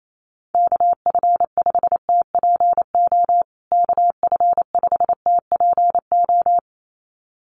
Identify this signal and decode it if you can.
Morse code, heard as a keyed tone, about 28 wpm, 700 Hz. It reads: KF5TPO KF5TPO
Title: USA Call Signs Repeated 2x 28wpm